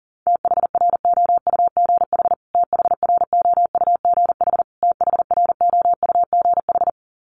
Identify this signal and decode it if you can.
Morse code, heard as a keyed tone, about 40 wpm, 700 Hz. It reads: THROUGH THROUGH THROUGH
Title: Top 500 Words Repeated 40wpm